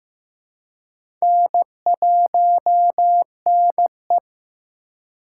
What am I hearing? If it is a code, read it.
Morse code, heard as a keyed tone, about 15 wpm, 700 Hz. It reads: N1NE
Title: USA Call Signs - 4 Characters - 3x Character Spacing - 15wpm